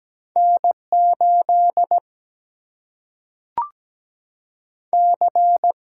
Morse code, heard as a keyed tone, about 17 wpm, 700 Hz. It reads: N8 C
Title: Single-Letter Single-Number 17wpm